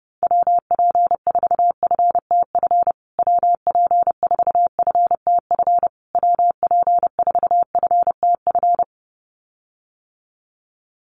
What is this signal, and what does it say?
Morse code, heard as a keyed tone, about 30 wpm, 700 Hz. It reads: WP4FTF WP4FTF WP4FTF
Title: USA Call Signs Repeated 3x 30wpm